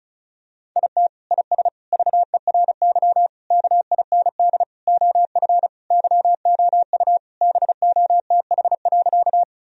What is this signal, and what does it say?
Morse code, heard as a keyed tone, about 36 wpm, 700 Hz. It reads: IT IS VERY KIND OF YOU BOTH.
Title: Sentences from Top 1000 Words - Review Part 1 - 35wpm